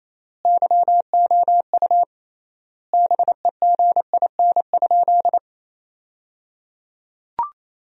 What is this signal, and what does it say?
Morse code, heard as a keyed tone, about 28 wpm, 700 Hz. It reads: YOU BEGIN?
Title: 3x Word-Spacing - Sentences from Top 300 Words - 28wpm